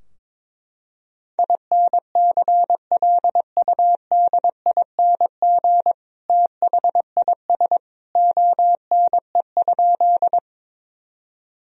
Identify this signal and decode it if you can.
Morse code, heard as a keyed tone, about 22 wpm, 700 Hz. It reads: INCLUDING THIS ONE?
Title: Sentences from Top 1000 Words - Review Part 4 - 22wpm